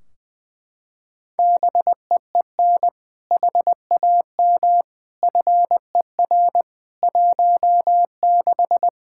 Morse code, heard as a keyed tone, about 20 wpm, 700 Hz. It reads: BEEN HAM FER 16
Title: Practice QSO - Exchange 3 - More Info - 20wpm